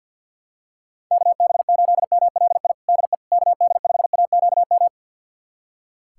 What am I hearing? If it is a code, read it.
Morse code, heard as a keyed tone, about 47 wpm, 700 Hz. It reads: KB8MLI DE KD5AQM
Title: USA Callsign DE Callsign 50wpm